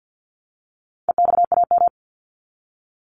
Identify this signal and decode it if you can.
Morse code, heard as a keyed tone, about 50 wpm, 700 Hz. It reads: EXAM
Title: Top 501-750 Words 50wpm